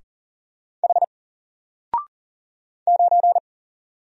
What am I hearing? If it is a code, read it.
Morse code, heard as a keyed tone, about 40 wpm, 700 Hz. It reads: H 9
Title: Single Letter-Number Alt 40wpm